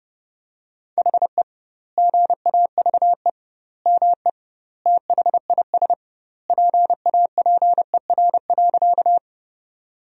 Morse code, heard as a keyed tone, about 30 wpm, 700 Hz. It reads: HE GAVE ME THIS PAPER.
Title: 2x Word-Spacing - Sentences from Top 500 Words - 30wpm